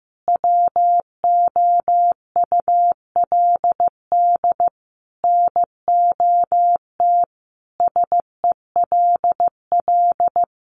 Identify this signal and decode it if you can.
Morse code, heard as a keyed tone, about 15 wpm, 700 Hz. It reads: WOULD NOT SELL
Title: Sentences from Top 1100 Words 15wpm